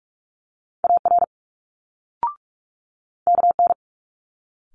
Morse code, heard as a keyed tone, about 46 wpm, 700 Hz. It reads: AR KN
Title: Multi-letter Phonemes Quick Fire 45wpm